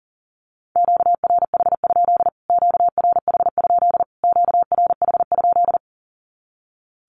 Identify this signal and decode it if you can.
Morse code, heard as a keyed tone, about 40 wpm, 700 Hz. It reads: QRH? QRH? QRH?
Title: Q-Codes Repeated 3x 40wpm